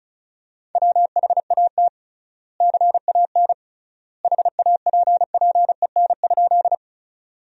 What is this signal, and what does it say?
Morse code, heard as a keyed tone, about 34 wpm, 700 Hz. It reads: WHAT CAN HAPPEN?
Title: 3x Word-Spacing - Sentences from Top 400 Words - 35wpm